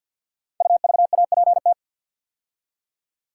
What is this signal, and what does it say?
Morse code, heard as a keyed tone, about 51 wpm, 700 Hz. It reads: UVAPT
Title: Code Groups - 5 Letter:Numbers - 50wpm